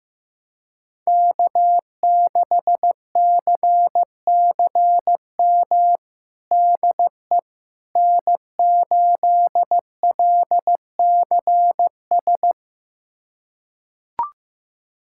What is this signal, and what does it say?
Morse code, heard as a keyed tone, about 15 wpm, 700 Hz. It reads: K6CCM DE N8LCS
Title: USA Callsign DE Callsign - 5 Characters - 15wpm